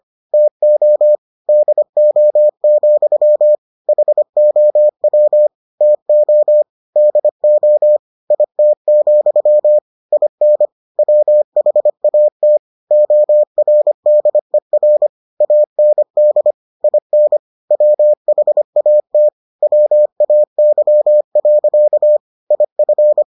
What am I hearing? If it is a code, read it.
Morse code, heard as a keyed tone, about 25 wpm, 600 Hz. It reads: TO DO, HOW TO DO IT, IN WHAT ORDER AND IN WHAT WAY. IF